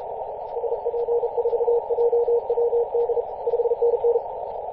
Morse code, heard as a keyed tone, about 34 wpm, 470 Hz, with radio noise. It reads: SP4JWD 5NN
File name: posts / 2016-may-01-0643z-a91hi-21-040mhz-cw